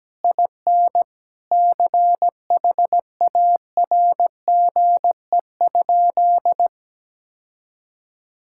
Morse code, heard as a keyed tone, about 17 wpm, 700 Hz. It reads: IN CHARGE?
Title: Sentences from Top 1000 Words - Review Part 6 - 17wpm